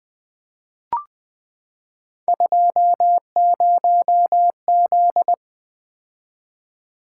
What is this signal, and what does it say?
Morse code, heard as a keyed tone, about 20 wpm, 700 Hz. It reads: 20Z
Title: Three Letter-Number Combinations - 20wpm